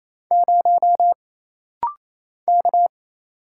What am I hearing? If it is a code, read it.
Morse code, heard as a keyed tone, about 27 wpm, 700 Hz. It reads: 0 K
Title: Single Letter-Number Alt - Rapid-Fire 28wpm